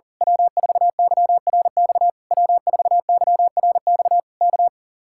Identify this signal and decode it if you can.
Morse code, heard as a keyed tone, about 38 wpm, 700 Hz. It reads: W4YRX W4YRX K
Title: Speed-Racing Common QSO Phrases - 40wpm